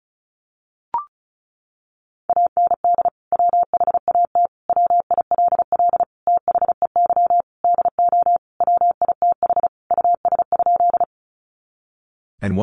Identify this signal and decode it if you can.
Morse code, heard as a keyed tone, about 34 wpm, 700 Hz. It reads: AND WHAT WILL THEY DO WITH US?
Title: Sentences from Top 200 Words 35wpm